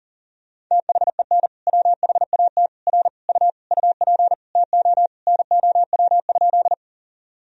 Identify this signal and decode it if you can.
Morse code, heard as a keyed tone, about 40 wpm, 700 Hz. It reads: THEN WHAT R U UP TO NOW?
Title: Sentences from Top 100 Words and Abbreviations 40wpm